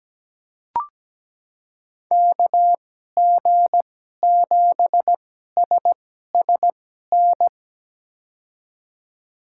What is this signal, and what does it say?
Morse code, heard as a keyed tone, about 17 wpm, spaced as for 12 wpm, 700 Hz. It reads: KG7SSN
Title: USA Call Signs - 6 Characters - 2x Character Spacing - 17wpm